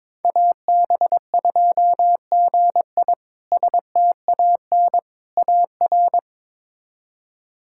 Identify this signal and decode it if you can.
Morse code, heard as a keyed tone, about 22 wpm, 700 Hz. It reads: AB2GI STAN AR
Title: K1USN SST QSO Practice - 22wpm